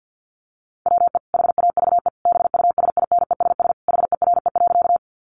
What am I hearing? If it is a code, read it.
Morse code, heard as a keyed tone, about 50 wpm, 700 Hz. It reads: WE HAVE BUSINESS HERE.
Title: Sentences from Top 600 Words 50wpm